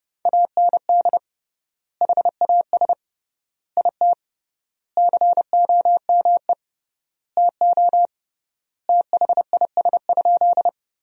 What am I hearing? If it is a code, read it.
Morse code, heard as a keyed tone, about 30 wpm, 700 Hz. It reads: AND HAS IT COME TO THIS?
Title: 3x Word-Spacing - Sentences from Top 100 Words - 30wpm